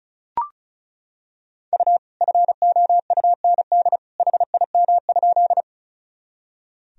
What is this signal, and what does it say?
Morse code, heard as a keyed tone, about 36 wpm, 700 Hz. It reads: U FOUND HIM?
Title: Sentences from Top 300 Words and Abbreviations 35wpm